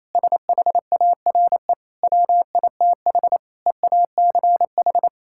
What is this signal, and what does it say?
Morse code, heard as a keyed tone, about 28 wpm, 700 Hz. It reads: SHARE WITH EACH
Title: Sentences from Top 1000 Words - Review Part 6 - 28wpm